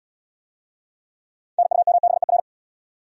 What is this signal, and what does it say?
Morse code, heard as a keyed tone, about 68 wpm, 700 Hz. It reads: NUMBER